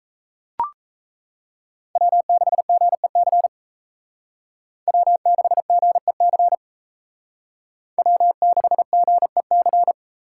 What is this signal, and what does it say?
Morse code, heard as a keyed tone, about 35 wpm, 700 Hz. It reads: W6GEC W6GEC W6GEC